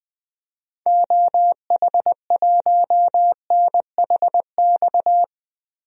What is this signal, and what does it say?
Morse code, heard as a keyed tone, about 21 wpm, 700 Hz. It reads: OH1NHX
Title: DXCC-100 Call Signs - 6 Characters - Repeated 3x - 20wpm